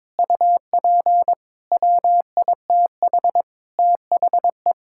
Morse code, heard as a keyed tone, about 22 wpm, 700 Hz. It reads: UP WITH THE
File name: Sentences with 7-Word Phrases - 22wpm